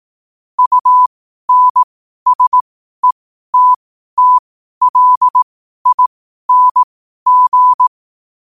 Morse code, heard as a keyed tone, about 18 wpm, spaced as for 12 wpm, 1000 Hz. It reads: UNSETTLING
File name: qotdfiles / 12wpm